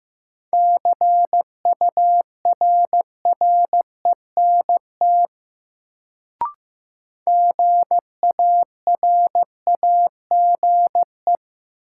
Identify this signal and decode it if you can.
Morse code, heard as a keyed tone, about 15 wpm, 700 Hz. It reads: CURRENT GARAGE
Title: Sentences from Top 1600 Words - New Word Drill Alt - 15wpm